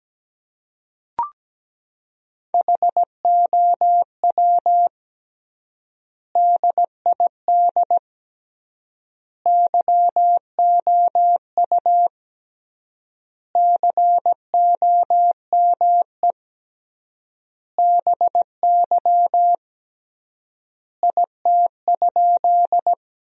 Morse code, heard as a keyed tone, about 17 wpm, 700 Hz. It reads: HOW DID YOU COME BY IT?
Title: 3x Word-Spacing - Sentences from Top 100 Words - 17wpm